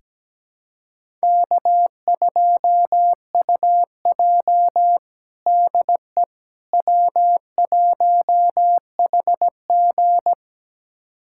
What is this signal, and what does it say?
Morse code, heard as a keyed tone, about 17 wpm, 700 Hz. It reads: K2UJ DE W1HG
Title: USA Callsign DE Callsign - 4 Characters - 17wpm